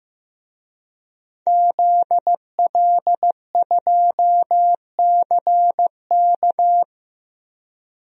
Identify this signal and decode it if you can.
Morse code, heard as a keyed tone, about 15 wpm, 700 Hz. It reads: ZL2CK
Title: DXCC-100 Call Signs - 5 Characters - Repeated 3x - 15wpm